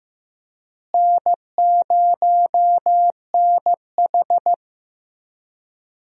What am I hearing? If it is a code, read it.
Morse code, heard as a keyed tone, about 15 wpm, 700 Hz. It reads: N0NH
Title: USA Call Signs - 4 Characters - Repeated 2x - 15wpm